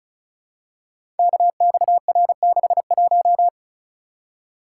Morse code, heard as a keyed tone, about 34 wpm, 700 Hz. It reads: KXR61
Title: Code Groups - 5 Letter:Numbers - 35wpm